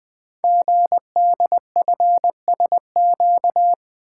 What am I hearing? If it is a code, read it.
Morse code, heard as a keyed tone, about 20 wpm, 700 Hz. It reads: GDFSQ
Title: Code Groups - 5 Letters - 20wpm